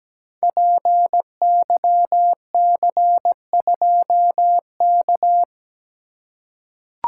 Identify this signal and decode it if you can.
Morse code, heard as a keyed tone, about 17 wpm, 700 Hz. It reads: PYC2K
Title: Code Groups - 5 Letter:Numbers - 17wpm